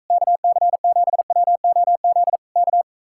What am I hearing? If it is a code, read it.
Morse code, heard as a keyed tone, about 45 wpm, 700 Hz. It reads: KC7WOZ K